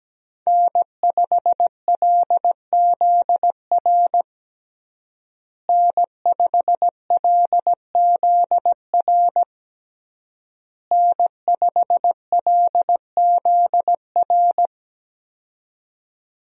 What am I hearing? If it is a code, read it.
Morse code, heard as a keyed tone, about 17 wpm, 700 Hz. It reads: N5LZR N5LZR N5LZR
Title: USA Call Signs - 5 Characters - Repeated 3x - 17wpm